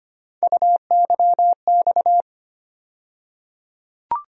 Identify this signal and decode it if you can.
Morse code, heard as a keyed tone, about 25 wpm, 700 Hz. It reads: UYX
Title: USA Call Sign Postfixes - 3 Letters 25wpm